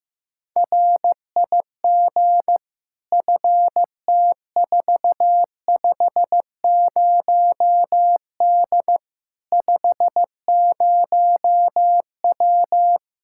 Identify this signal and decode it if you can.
Morse code, heard as a keyed tone, about 15 wpm, 700 Hz. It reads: RIG FT450D 50W
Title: Practice QSO - Exchange 2 - Info - 15wpm